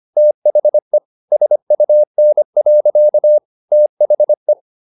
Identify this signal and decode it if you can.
Morse code, heard as a keyed tone, about 25 wpm, 600 Hz. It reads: THE SUN. THE
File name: the old man and the sea part 1 25wpm